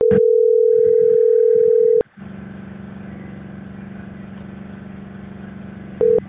Ringback tone (North America), 440 + 480 Hz.